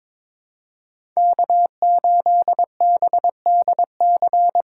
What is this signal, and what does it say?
Morse code, heard as a keyed tone, about 22 wpm, 700 Hz. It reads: K8BDC
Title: USA Call Signs - 5 Characters - 6x Character Spacing - 22wpm